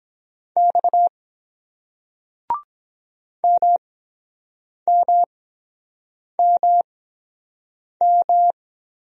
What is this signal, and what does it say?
Morse code, heard as a keyed tone, about 23 wpm, spaced as for 7 wpm, 700 Hz. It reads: X MMMM